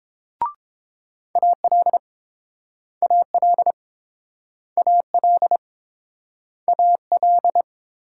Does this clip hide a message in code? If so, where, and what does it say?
Yes: Morse code from 1.4 to 7.6 s, heard as a keyed tone, about 30 wpm, 700 Hz. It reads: AL AL AL AL